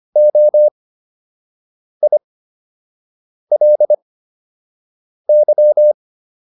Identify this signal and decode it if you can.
Morse code, heard as a keyed tone, about 25 wpm, spaced as for 6 wpm, 600 Hz. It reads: OILY